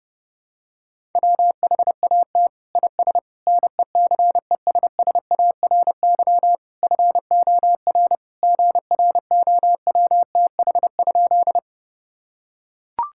Morse code, heard as a keyed tone, about 32 wpm, 700 Hz. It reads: WHAT IS NECESSARY FOR GROWTH?
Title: Sentences from Top 1000 Words - Review Part 5 - 30wpm